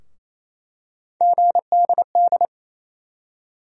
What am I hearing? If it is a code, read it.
Morse code, heard as a keyed tone, about 27 wpm, 700 Hz. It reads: GDD
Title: USA Call Sign Postfixes - 3 Letters 28wpm